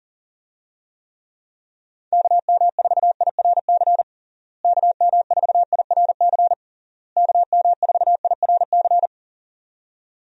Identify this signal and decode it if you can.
Morse code, heard as a keyed tone, about 40 wpm, 700 Hz. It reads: KM4IRC KM4IRC KM4IRC
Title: USA Call Signs - 6 Characters - Repeated 3x - 40wpm